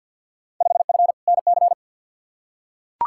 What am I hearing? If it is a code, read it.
Morse code, heard as a keyed tone, about 52 wpm, 700 Hz. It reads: 5F NC
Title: Field Day Exchanges Encore - 50wpm